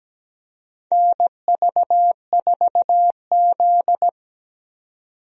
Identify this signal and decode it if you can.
Morse code, heard as a keyed tone, about 17 wpm, 700 Hz. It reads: NV4Z
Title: USA Call Signs - 4 Characters - 17wpm